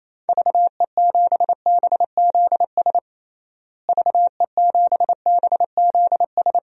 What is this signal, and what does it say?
Morse code, heard as a keyed tone, about 27 wpm, 700 Hz. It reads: VE7BZS VE7BZS
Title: DXCC-100 Call Signs - 6 Characters - Repeated 2x - 28wpm